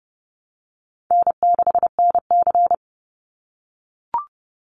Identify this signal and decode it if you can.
Morse code, heard as a keyed tone, about 30 wpm, 700 Hz. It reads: N6NC
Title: Easy Call Signs - 30wpm